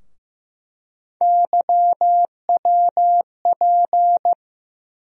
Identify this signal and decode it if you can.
Morse code, heard as a keyed tone, about 15 wpm, 700 Hz. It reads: YWP
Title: USA Call Sign Postfixes - 3 Letters 15wpm